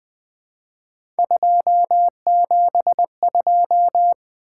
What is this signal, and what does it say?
Morse code, heard as a keyed tone, about 20 wpm, 700 Hz. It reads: 272
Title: Three Letter-Number Combinations - 20wpm